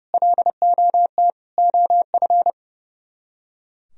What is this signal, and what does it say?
Morse code, heard as a keyed tone, about 39 wpm, 700 Hz. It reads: LOT OF